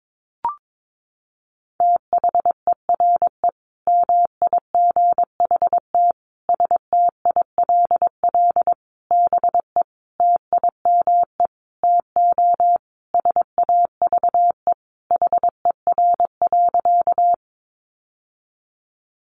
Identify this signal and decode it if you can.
Morse code, heard as a keyed tone, about 22 wpm, 700 Hz. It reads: THERE MIGHT STILL BE TIME TO SAVE HER.
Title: Sentences from Top 1100 Words 22wpm